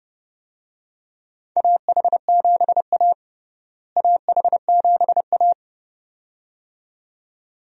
Morse code, heard as a keyed tone, about 30 wpm, 700 Hz. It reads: AH7A AH7A
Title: USA Call Signs - 4 Characters - Repeated 2x - 30wpm